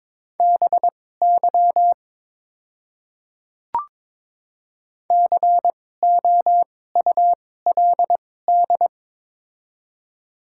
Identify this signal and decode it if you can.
Morse code, heard as a keyed tone, about 22 wpm, spaced as for 16 wpm, 700 Hz. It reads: BY COULD
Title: 2x Character Spacing - Top 200 Words - 22wpm